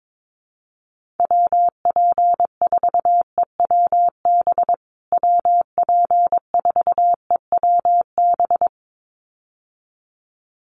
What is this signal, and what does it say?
Morse code, heard as a keyed tone, about 22 wpm, 700 Hz. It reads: WP4EWB WP4EWB
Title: USA Call Signs Repeated 2x 22wpm